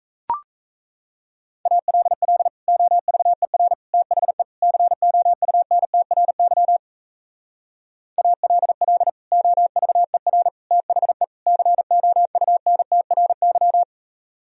Morse code, heard as a keyed tone, about 36 wpm, 700 Hz. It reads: ALL OVER THE COUNTRY ALL OVER THE COUNTRY